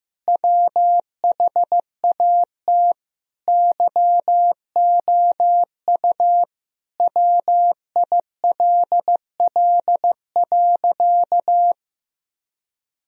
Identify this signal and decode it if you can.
Morse code, heard as a keyed tone, about 15 wpm, 700 Hz. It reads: WHAT YOU WILL.
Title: Sentences from Top 1000 Words - Review Part 2 - 15wpm